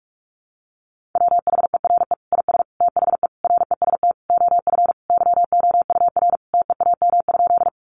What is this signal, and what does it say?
Morse code, heard as a keyed tone, about 44 wpm, 700 Hz. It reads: WHERE IS THE REST OF YOUR TEAM?
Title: Sentences from Top 1500 Words 45wpm